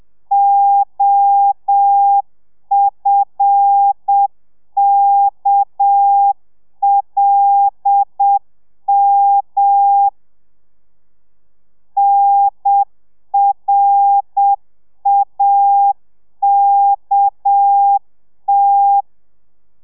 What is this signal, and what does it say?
Morse code, heard as a keyed tone, about 7 wpm, 800 Hz. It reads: OFKLM NRAKT